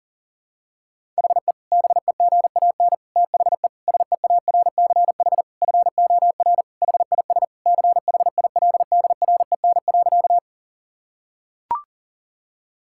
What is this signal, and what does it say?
Morse code, heard as a keyed tone, about 40 wpm, 700 Hz. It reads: HE BEGAN THE SEARCH FOR HIS CHILDREN.
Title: Sentences from Top 1000 Words - Review Part 5 - 40wpm